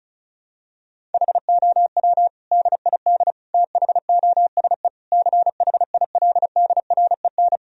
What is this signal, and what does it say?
Morse code, heard as a keyed tone, about 34 wpm, 700 Hz. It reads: HOW DID THOSE CHILDREN
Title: Sentences from Top 1000 Words - Review Part 1 - 35wpm